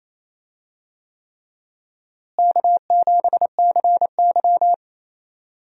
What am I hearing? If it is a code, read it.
Morse code, heard as a keyed tone, about 28 wpm, 700 Hz. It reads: K7CY
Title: USA Call Signs - 4 Characters - Repeated 3x - 28wpm